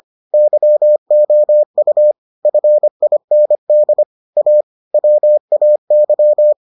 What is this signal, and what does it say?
Morse code, heard as a keyed tone, about 25 wpm, 600 Hz. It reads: YOU FIND A WAY